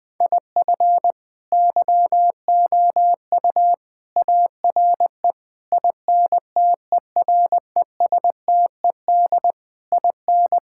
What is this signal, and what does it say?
Morse code, heard as a keyed tone, about 21 wpm, 700 Hz. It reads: IF YOU ARE INTERESTED IN
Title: Sentences with 8-Word Phrases - Spoken Beginning - 20wpm